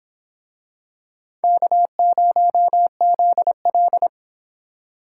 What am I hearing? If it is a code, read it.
Morse code, heard as a keyed tone, about 26 wpm, 700 Hz. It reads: K0ZL